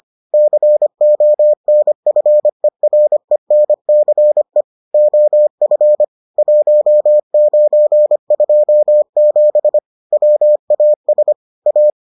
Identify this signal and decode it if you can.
Morse code, heard as a keyed tone, about 25 wpm, 600 Hz. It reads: CONFERENCE OF 1927 WAS A